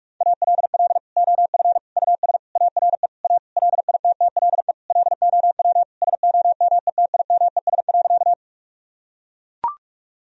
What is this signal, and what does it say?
Morse code, heard as a keyed tone, about 41 wpm, 700 Hz. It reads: ALL OF US ARE A LITTLE LOW SOMETIMES.
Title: Sentences from Top 1000 Words - Review Part 3 - 45wpm